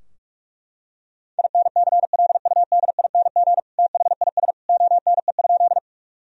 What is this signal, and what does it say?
Morse code, heard as a keyed tone, about 41 wpm, 700 Hz. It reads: INCLUDING THIS ONE?
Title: Sentences from Top 1000 Words - Review Part 4 - 45wpm